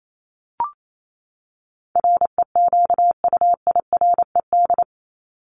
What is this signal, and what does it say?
Morse code, heard as a keyed tone, about 28 wpm, 700 Hz. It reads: REQUIRED
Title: ED Suffix - Top 100 Words with Base Words 28wpm